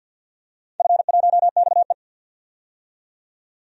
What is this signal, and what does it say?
Morse code, heard as a keyed tone, about 51 wpm, 700 Hz. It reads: F1XE
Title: DXCC-100 Call Signs - 4 Characters - 50wpm